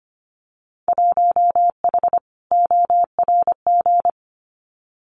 Morse code, heard as a keyed tone, about 25 wpm, 700 Hz. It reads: 1H ORG
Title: Winter Field Day Exchanges 25wpm